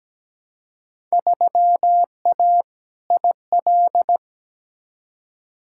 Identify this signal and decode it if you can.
Morse code, heard as a keyed tone, about 17 wpm, 700 Hz. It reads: 3A IL
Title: Field Day Exchanges - 17wpm